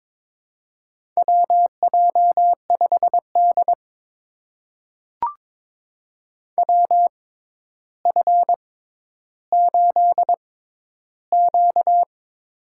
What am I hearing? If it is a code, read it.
Morse code, heard as a keyed tone, about 22 wpm, spaced as for 11 wpm, 700 Hz. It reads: WJ5D WF8Q
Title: USA Call Signs - 4 Characters - 6x Character Spacing - 22wpm